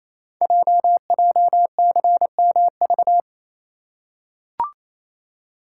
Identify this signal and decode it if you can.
Morse code, heard as a keyed tone, about 27 wpm, 700 Hz. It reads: JJCMV
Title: Code Groups - 5 Letters - 28wpm